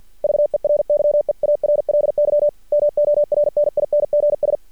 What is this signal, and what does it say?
Morse code, heard as a keyed tone, about 53 wpm, 590 Hz. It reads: VERY EARLY MORNINGS
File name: archived / CRSnet